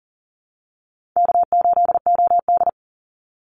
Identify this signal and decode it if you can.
Morse code, heard as a keyed tone, about 40 wpm, 700 Hz. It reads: K8OD
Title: Easy Call Signs Repeated 2x - 40wpm